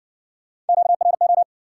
Morse code, heard as a keyed tone, about 60 wpm, 700 Hz. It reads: QRO